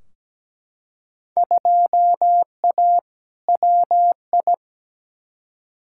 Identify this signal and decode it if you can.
Morse code, heard as a keyed tone, about 17 wpm, 700 Hz. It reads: 2A WI
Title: Field Day Exchanges Encore - 17wpm